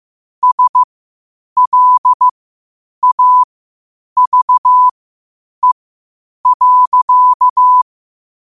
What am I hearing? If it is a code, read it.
Morse code, heard as a keyed tone, about 15 wpm, spaced as for 8 wpm, 1000 Hz. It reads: SLAVE.
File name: qotdfiles / 8wpm